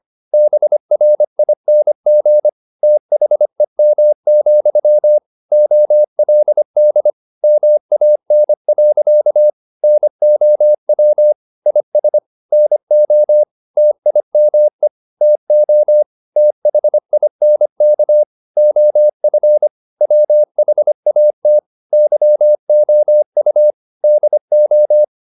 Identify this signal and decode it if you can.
Morse code, heard as a keyed tone, about 25 wpm, 600 Hz. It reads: BRING THEM, OLD MAN. NOW IS NO TIME TO THINK OF WHAT YOU DO